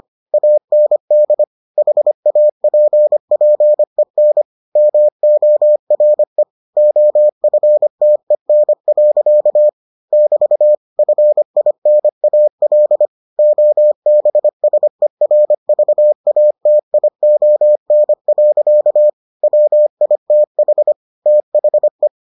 Morse code, heard as a keyed tone, about 25 wpm, 600 Hz. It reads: AND HAPPEN MORE OFTEN. <BT> FINAL OBSERVATION. WITH THE